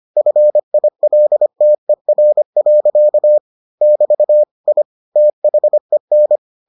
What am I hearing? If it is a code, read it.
Morse code, heard as a keyed tone, about 25 wpm, 600 Hz. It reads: FILTER. <BT> I THEN